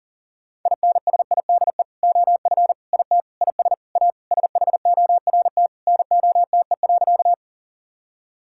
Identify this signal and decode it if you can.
Morse code, heard as a keyed tone, about 40 wpm, 700 Hz. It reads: INSIDE OF IT IS A SHORT NOTE.
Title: Sentences from Top 1000 Words - Review Part 6 - 40wpm